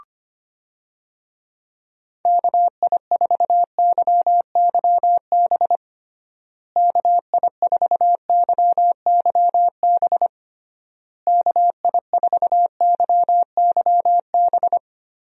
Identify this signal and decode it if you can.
Morse code, heard as a keyed tone, about 25 wpm, 700 Hz. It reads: KI4YYB KI4YYB KI4YYB
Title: USA Call Signs - 6 Characters - Repeated 3x - 25wpm